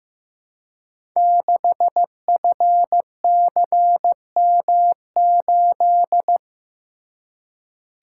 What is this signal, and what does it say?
Morse code, heard as a keyed tone, about 15 wpm, 700 Hz. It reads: 6FCM8
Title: Code Groups - 5 Letter:Numbers - 15wpm